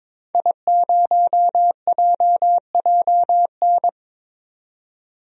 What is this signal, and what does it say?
Morse code, heard as a keyed tone, about 22 wpm, 700 Hz. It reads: I0JJN
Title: DXCC-100 Call Signs - 5 Characters - 22wpm